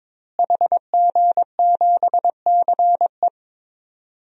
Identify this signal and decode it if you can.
Morse code, heard as a keyed tone, about 22 wpm, 700 Hz. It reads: HG7CE
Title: DXCC-100 Call Signs - 5 Characters - Repeated 3x - 22wpm